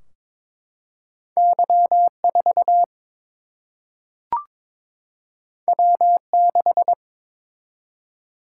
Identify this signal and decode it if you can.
Morse code, heard as a keyed tone, about 22 wpm, 700 Hz. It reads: Y 4 W 6
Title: Single-Letter Single-Number 22wpm